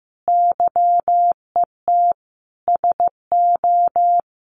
Morse code, heard as a keyed tone, about 15 wpm, 700 Hz. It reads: YET SO
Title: Sentences from Top 500 Words 15wpm